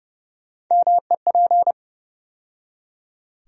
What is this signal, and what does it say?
Morse code, heard as a keyed tone, about 30 wpm, 700 Hz. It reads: MEP
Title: USA Call Sign Postfixes - 3 Letters 30wpm